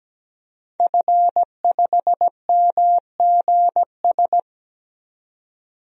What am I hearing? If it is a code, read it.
Morse code, heard as a keyed tone, about 17 wpm, 700 Hz. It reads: F5MGS
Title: DXCC-100 Call Signs - 5 Characters - Repeated 3x - 17wpm